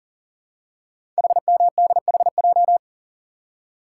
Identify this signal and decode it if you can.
Morse code, heard as a keyed tone, about 38 wpm, 700 Hz. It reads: HMDHJ
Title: Code Groups - 5 Letters - 40wpm